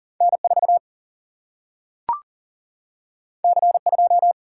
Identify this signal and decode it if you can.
Morse code, heard as a keyed tone, about 40 wpm, 700 Hz. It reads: N 4 C 2
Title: Single-Letter Single-Number 40wpm